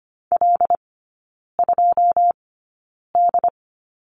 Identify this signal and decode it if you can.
Morse code, heard as a keyed tone, about 25 wpm, spaced as for 9 wpm, 690 Hz. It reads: L2D
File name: Lesson 15e - Number 5 - Three Characters 25wpm